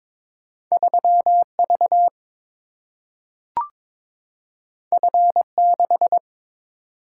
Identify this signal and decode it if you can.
Morse code, heard as a keyed tone, about 22 wpm, 700 Hz. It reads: 3 V F 6
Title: Single-Letter Single-Number 22wpm